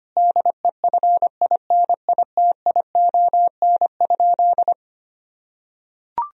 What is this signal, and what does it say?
Morse code, heard as a keyed tone, about 25 wpm, 700 Hz. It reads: DEFINITION?
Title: Sentences from Top 1000 Words - Review Part 4 - 25wpm